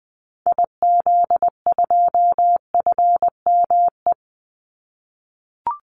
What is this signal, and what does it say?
Morse code, heard as a keyed tone, about 20 wpm, 700 Hz. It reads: IZ2FME
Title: Call Signs - 20wpm